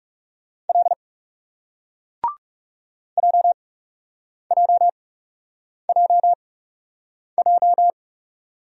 Morse code, heard as a keyed tone, about 39 wpm, spaced as for 8 wpm, 700 Hz. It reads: L JJJJ